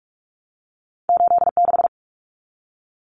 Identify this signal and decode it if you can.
Morse code, heard as a keyed tone, about 46 wpm, 700 Hz. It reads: 86
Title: Common Cut Numbers - 2-Digit 45wpm